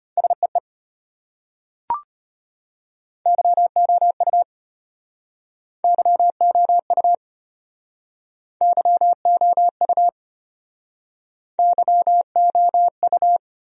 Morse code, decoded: SEE YOU YOU YOU YOU

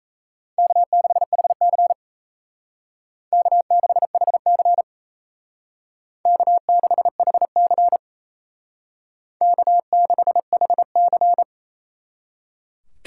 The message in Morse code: K6HC K6HC K6HC K6HC

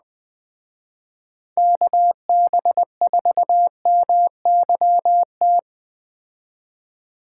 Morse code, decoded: KB4MYT

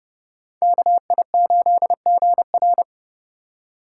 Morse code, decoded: KI8GR

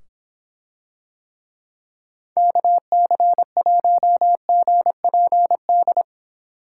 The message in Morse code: KC1GPD

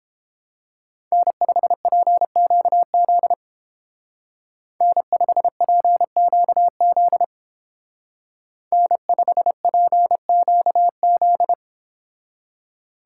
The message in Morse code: N5PQZ N5PQZ N5PQZ

